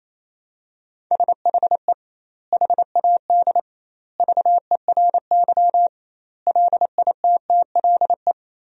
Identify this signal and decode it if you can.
Morse code, heard as a keyed tone, about 28 wpm, 700 Hz. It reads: SHE HAD VERY LITTLE
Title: 2x Word-Spacing - Sentences from Top 400 Words - 28wpm